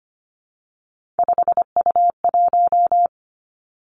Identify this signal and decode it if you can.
Morse code, heard as a keyed tone, about 25 wpm, 700 Hz. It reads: 5U1
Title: Three Challenging Letter-Number Combinations 25wpm